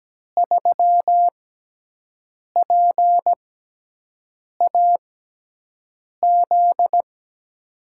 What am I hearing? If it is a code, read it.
Morse code, heard as a keyed tone, about 17 wpm, spaced as for 6 wpm, 700 Hz. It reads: 3PAZ
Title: USA Call Signs - 5 Characters - 6x Character Spacing - 17wpm